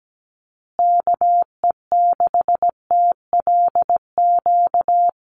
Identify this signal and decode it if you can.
Morse code, heard as a keyed tone, about 17 wpm, 700 Hz. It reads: KE6TLQ
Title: USA Call Signs Repeated 2x 17wpm